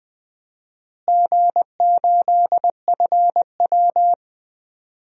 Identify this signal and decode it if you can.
Morse code, heard as a keyed tone, about 20 wpm, 700 Hz. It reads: G8FW
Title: DXCC-100 Call Signs - 4 Characters - Repeated 3x - 20wpm